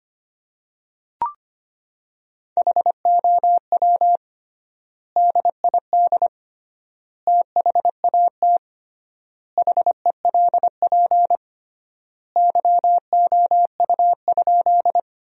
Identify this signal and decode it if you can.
Morse code, heard as a keyed tone, about 25 wpm, 700 Hz. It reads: HOW DID THAT HELP YOU?
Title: 3x Word-Spacing - Sentences from Top 200 Words - 25wpm